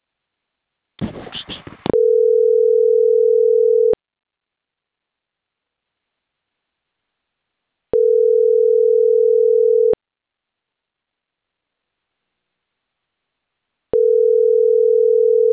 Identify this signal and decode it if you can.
Ringback tone (North America), 440 + 480 Hz.